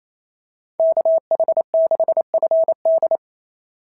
Morse code, heard as a keyed tone, about 28 wpm, 690 Hz. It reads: KH6FD
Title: SOTA QSO Practice 28wpm